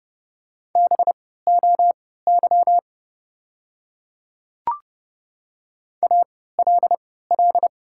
Morse code, heard as a keyed tone, about 30 wpm, spaced as for 17 wpm, 690 Hz. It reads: BOY ALL
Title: 3x Character Spacing - Top 300 Words - 30wpm